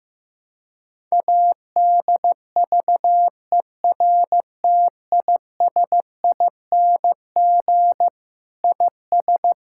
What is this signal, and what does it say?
Morse code, heard as a keyed tone, about 14 wpm, 700 Hz. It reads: ADVERTISING IS
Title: Sentences from Top 1000 Words - Review Part 6 - 15wpm